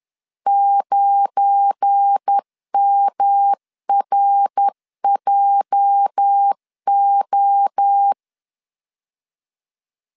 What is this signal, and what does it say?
Morse code, heard as a keyed tone, about 10 wpm, 800 Hz. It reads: 9MRJO
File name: LEZIONI / 8 YJ45 / 8 YJ 10wpm 2651